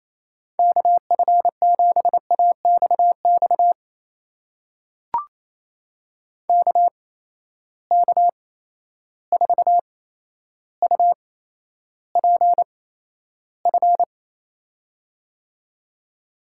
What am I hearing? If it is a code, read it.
Morse code, heard as a keyed tone, about 27 wpm, spaced as for 12 wpm, 700 Hz. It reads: KF7AXX KK4UPF
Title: USA Call Signs - 6 Characters - 8x Character Spacing - 28wpm